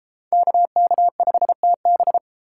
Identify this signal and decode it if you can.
Morse code, heard as a keyed tone, about 31 wpm, 700 Hz. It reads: KK5TB